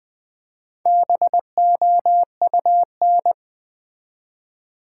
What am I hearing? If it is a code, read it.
Morse code, heard as a keyed tone, about 20 wpm, 700 Hz. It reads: BOUN
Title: Build A Word - 1 Letter At A Time - 5-Letter Words - 20wpm